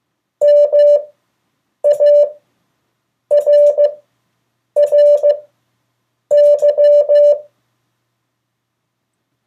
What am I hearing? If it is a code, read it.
Morse code, heard as a keyed tone, about 15 wpm, spaced as for 7 wpm, 580 Hz. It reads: MARRY
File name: 2of4-words-1